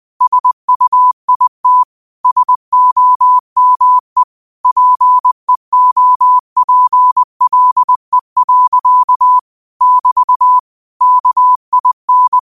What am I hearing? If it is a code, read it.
Morse code, heard as a keyed tone, about 20 wpm, 1000 Hz. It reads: SUIT SOME PEOPLE. <BT> KIN
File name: qotdfiles / 20wpm